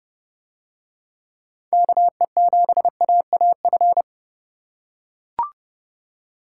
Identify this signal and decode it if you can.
Morse code, heard as a keyed tone, about 30 wpm, 700 Hz. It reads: KE7AAF